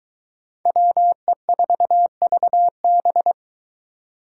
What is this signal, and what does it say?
Morse code, heard as a keyed tone, about 20 wpm, 700 Hz. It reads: WE4VB
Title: Speed Racing - USA 2x2 Call Signs 17wpm